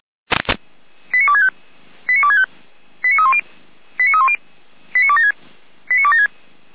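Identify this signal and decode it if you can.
Five-tone selective call (ZVEI), heard as quick runs of five tones, tones of about 70 ms. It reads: 89265 89265 89310 89310 89276 89276
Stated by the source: ZVEI five-tone selective call 89174 received by SDR.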